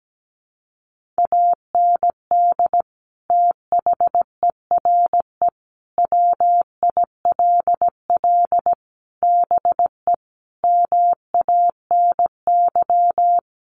Morse code, heard as a keyed tone, about 17 wpm, 700 Hz. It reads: AND THERE WILL BE MANY
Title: Sentences from Top 1500 Words 17wpm